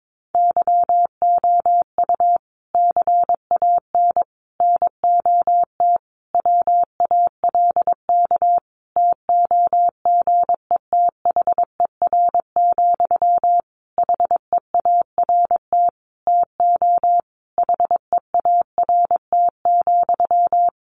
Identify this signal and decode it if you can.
Morse code, heard as a keyed tone, about 22 wpm, 700 Hz. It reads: YOU CAN NOT WALK TOGETHER, HEART TO HEART,
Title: Sentences from Top 1500 Words 22wpm